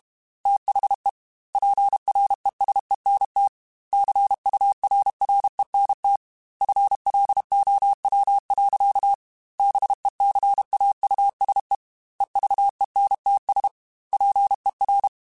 Morse code, heard as a keyed tone, about 32 wpm, 800 Hz. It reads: THE PRESENT CURRENT FLOW. BECAUSE EVENTS PER
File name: Technical Subjects / Technical Subjects-mp3s / XL @30